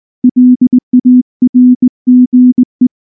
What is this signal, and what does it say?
Morse code, heard as a keyed tone, about 19 wpm, 250 Hz. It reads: LARGE
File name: morse / morse-training-20WPM-3